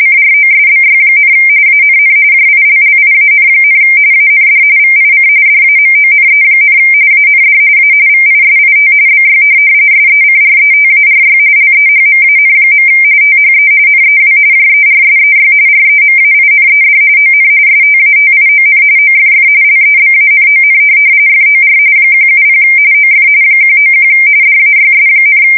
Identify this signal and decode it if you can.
RTTY, about 45.45 baud. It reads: CROSBY, ANN JILLIAN, AND VIC DAMONE, AND A VISIT BY SECRETARY LEHMAN. THE NEW JERSEY REMAINED ON STATION OFF LEBANON FOR 170 OF 191 DAYS,
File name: fsk170 / USS-New-Jersey